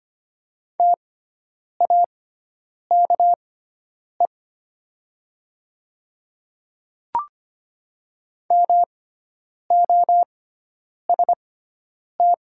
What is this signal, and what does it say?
Morse code, heard as a keyed tone, about 25 wpm, spaced as for 9 wpm, 700 Hz. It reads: TAKE MOST
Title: 6x Character Spacing - Top 300 Words - 25wpm